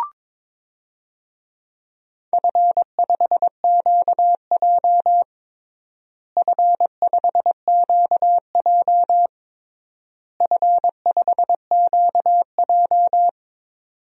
Morse code, heard as a keyed tone, about 22 wpm, 700 Hz. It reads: F5QJ F5QJ F5QJ
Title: DXCC-100 Call Signs - 4 Characters - Repeated 3x - 22wpm